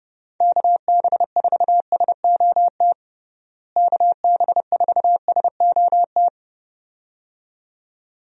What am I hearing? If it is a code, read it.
Morse code, heard as a keyed tone, about 30 wpm, 700 Hz. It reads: KB4SOT KB4SOT
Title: USA Call Signs - 6 Characters - Repeated 2x - 30wpm